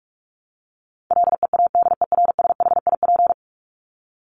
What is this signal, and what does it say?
Morse code, heard as a keyed tone, about 44 wpm, 700 Hz. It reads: LEADERSHIP
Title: Top 1501-1750 Words 45wpm